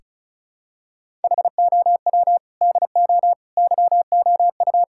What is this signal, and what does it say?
Morse code, heard as a keyed tone, about 34 wpm, 700 Hz. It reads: HOW DO YOU
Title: Sentences from Top 1000 Words - Review Part 6 - 35wpm